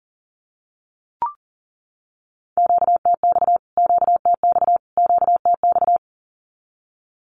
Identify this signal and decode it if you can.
Morse code, heard as a keyed tone, about 40 wpm, 700 Hz. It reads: QTX QTX QTX
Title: Q-Codes Repeated 3x 40wpm